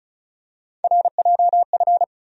Morse code, heard as a keyed tone, about 34 wpm, 700 Hz. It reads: RJF
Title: USA Call Sign Postfixes - 3 Letters 35wpm